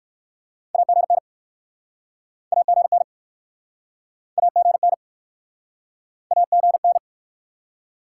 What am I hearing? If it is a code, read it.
Morse code, heard as a keyed tone, about 62 wpm, 700 Hz. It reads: AGN AGN AGN AGN